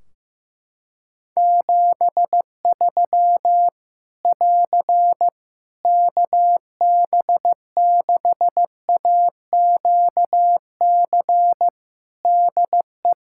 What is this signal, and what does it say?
Morse code, heard as a keyed tone, about 15 wpm, 700 Hz. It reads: 73 <AR> KB6AQC DE
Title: Practice QSO - Exchange 4 - Ending - 15wpm